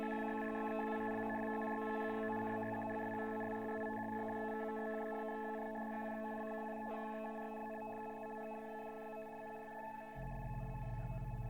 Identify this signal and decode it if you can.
Two-tone teleprinter signal, about 200 Hz shift.